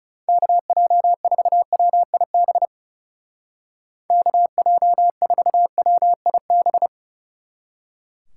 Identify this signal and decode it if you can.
Morse code, heard as a keyed tone, about 35 wpm, 700 Hz. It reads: KJ4WIB KJ4WIB
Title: Speed Racing - USA 2x3 Call Signs 30wpm